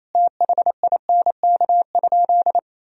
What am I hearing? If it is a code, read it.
Morse code, heard as a keyed tone, about 28 wpm, 700 Hz. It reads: THINK?
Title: Sentences from Top 1000 Words - Review Part 4 - 28wpm